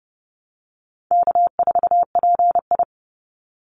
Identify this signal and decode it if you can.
Morse code, heard as a keyed tone, about 30 wpm, 700 Hz. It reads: K4PI
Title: Easy Call Signs Repeated 3x - 30wpm